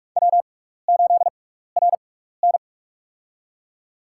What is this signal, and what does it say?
Morse code, heard as a keyed tone, about 46 wpm, spaced as for 16 wpm, 700 Hz. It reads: W8RN
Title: USA Call Signs - 4 Characters - 6x Character Spacing - 45wpm